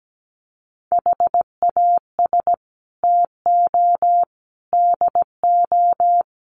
Morse code, heard as a keyed tone, about 17 wpm, 700 Hz. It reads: HAS TO DO
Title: Sets of 5 Words 17wpm